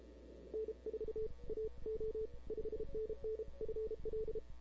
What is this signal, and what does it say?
Morse code, heard as a keyed tone, about 32 wpm, 500 Hz. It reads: N4AO 5NN FL
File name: CW / 2005-k5tr arrl cw dx -00:00